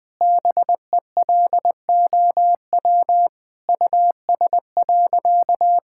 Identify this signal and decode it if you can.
Morse code, heard as a keyed tone, about 20 wpm, 700 Hz. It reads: BELOW US.
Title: Sentences from Top 1000 Words - Review Part 5 - 20wpm